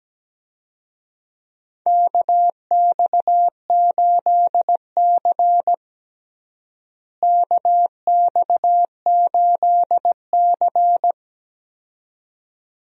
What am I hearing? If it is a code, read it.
Morse code, heard as a keyed tone, about 17 wpm, 700 Hz. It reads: KX8C KX8C
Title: USA Call Signs - 4 Characters - Repeated 2x - 17wpm